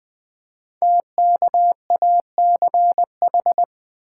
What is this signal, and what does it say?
Morse code, heard as a keyed tone, about 20 wpm, 700 Hz. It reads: TKACH
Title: Code Groups - 5 Letters - 20wpm